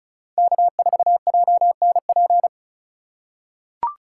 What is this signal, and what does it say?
Morse code, heard as a keyed tone, about 34 wpm, 700 Hz. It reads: K4JNP
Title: USA Call Signs - 2x Character Spacing - 35wpm